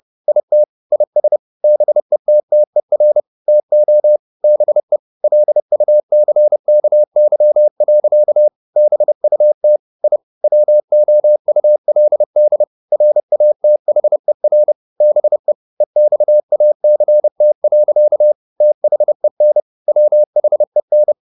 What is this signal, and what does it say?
Morse code, heard as a keyed tone, about 30 wpm, 600 Hz. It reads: IT IS BETTER TO BE LUCKY. BUT I WOULD RATHER BE EXACT. THEN WHEN
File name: the old man and the sea / the old man and the sea 1 30wpm